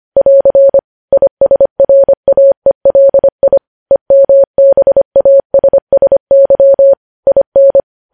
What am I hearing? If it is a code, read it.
Morse code, heard as a keyed tone, about 25 wpm, 550 Hz. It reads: <AR> ISRAELI EMBASSY IN
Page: news headlines in morse code at 25 wpm